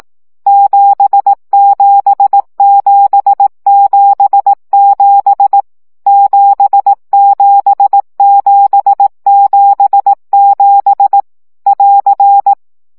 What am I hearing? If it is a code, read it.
Morse code, heard as a keyed tone, about 18 wpm, 800 Hz. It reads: 77777 77777 <AR>
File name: CW gyakorlas 7 15 baud